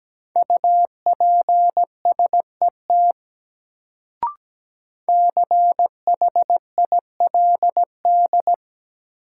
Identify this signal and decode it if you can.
Morse code, heard as a keyed tone, about 17 wpm, 700 Hz. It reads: UPSET CHILD
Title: Sentences from Top 800 Words - New Word Drill - 17wpm